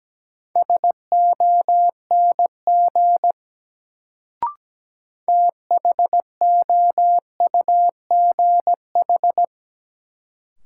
Morse code, heard as a keyed tone, about 17 wpm, 700 Hz. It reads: SONG THOUGH
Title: Sentences from Top 400 Words - New Word Drill - 17wpm